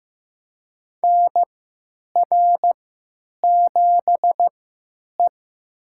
Morse code, heard as a keyed tone, about 15 wpm, spaced as for 9 wpm, 700 Hz. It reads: NR7E
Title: USA Call Signs - 4 Characters - 3x Character Spacing - 15wpm